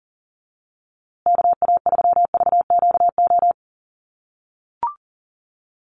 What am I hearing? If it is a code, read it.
Morse code, heard as a keyed tone, about 40 wpm, 700 Hz. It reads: KA3VQO
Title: USA Call Signs Repeated 2x 40wpm